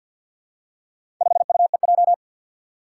Morse code, heard as a keyed tone, about 51 wpm, 700 Hz. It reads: 5UEJ
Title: Four Challenging Letter-Number Combinations 50wpm